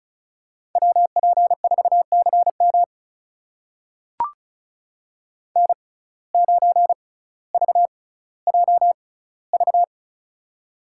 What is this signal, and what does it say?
Morse code, heard as a keyed tone, about 36 wpm, spaced as for 18 wpm, 700 Hz. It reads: WP4CM N9VJV
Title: USA Call Signs - 5 Characters - 6x Character Spacing - 35wpm